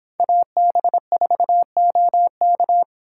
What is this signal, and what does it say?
Morse code, heard as a keyed tone, about 26 wpm, 700 Hz. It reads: AB4OK